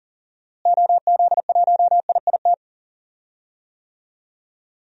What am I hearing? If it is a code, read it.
Morse code, heard as a keyed tone, about 40 wpm, 700 Hz. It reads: OZ1IIT
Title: DXCC-100 Call Signs - 6 Characters - Repeated 3x - 40wpm